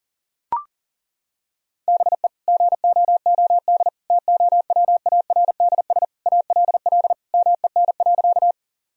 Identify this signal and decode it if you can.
Morse code, heard as a keyed tone, about 40 wpm, 700 Hz. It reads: BE GOOD TOWARDS ALL MEN.
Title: Sentences from Top 1000 Words - Review Part 2 - 40wpm